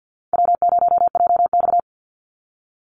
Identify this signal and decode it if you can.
Morse code, heard as a keyed tone, about 48 wpm, 700 Hz. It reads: W0JX
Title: Call Signs Repeated 3x - 50wpm